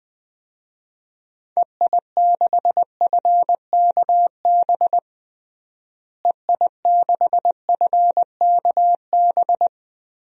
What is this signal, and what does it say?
Morse code, heard as a keyed tone, about 21 wpm, 700 Hz. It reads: EI6FKB EI6FKB
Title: DXCC-100 Call Signs - 6 Characters - Repeated 2x - 20wpm